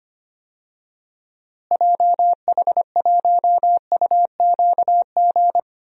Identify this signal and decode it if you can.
Morse code, heard as a keyed tone, about 25 wpm, 700 Hz. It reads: JH1UQG
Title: DXCC-100 Call Signs - 6 Characters - Repeated 3x - 25wpm